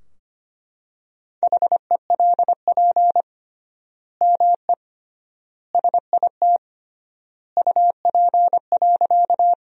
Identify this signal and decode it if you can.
Morse code, heard as a keyed tone, about 25 wpm, 700 Hz. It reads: HELP ME SIT UP.
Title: 3x Word-Spacing - Sentences from Top 500 Words - 25wpm